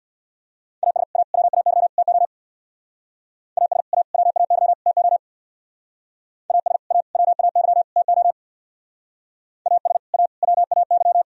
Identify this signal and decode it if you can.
Morse code, heard as a keyed tone, about 64 wpm, 700 Hz. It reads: AS A WAY TO AS A WAY TO AS A WAY TO AS A WAY